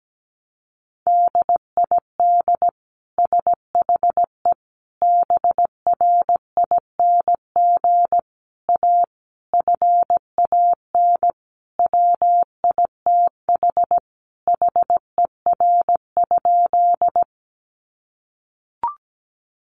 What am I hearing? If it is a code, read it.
Morse code, heard as a keyed tone, about 17 wpm, 700 Hz. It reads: DID SHE BRING A FAN WITH HER?
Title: Sentences from Top 1300 Words 17wpm